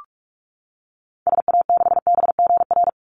Morse code, heard as a keyed tone, about 44 wpm, 700 Hz. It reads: SA6BGR
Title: Call Signs - 45wpm